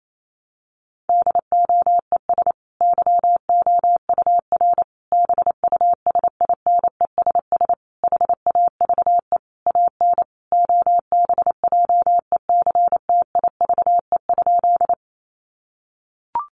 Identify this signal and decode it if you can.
Morse code, heard as a keyed tone, about 28 wpm, 700 Hz. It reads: DOES YOUR BUSINESS HAVE AN OBJECTIVE?
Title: Sentences from Top 1200 Words 28wpm